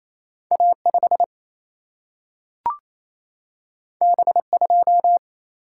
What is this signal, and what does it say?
Morse code, heard as a keyed tone, about 28 wpm, 700 Hz. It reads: A5 B2
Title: Single-Letter Single-Number 28wpm